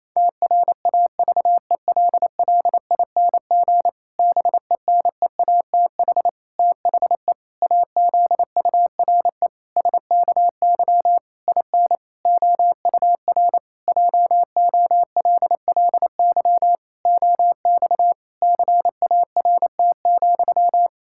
Morse code, heard as a keyed tone, about 28 wpm, 700 Hz. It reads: TRAVELLING BENEATH THE AZURE SKY IN OUR JOLLY OX CART,
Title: Pangram Challenge v2 - 28wpm